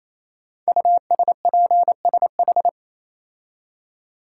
Morse code, heard as a keyed tone, about 27 wpm, 700 Hz. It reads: USPSH
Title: Code Groups - 5 Letters - 28wpm